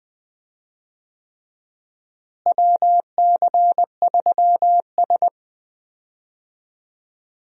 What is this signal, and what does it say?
Morse code, heard as a keyed tone, about 21 wpm, 700 Hz. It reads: WC3S
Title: USA Call Signs - 4 Characters - Repeated 3x - 20wpm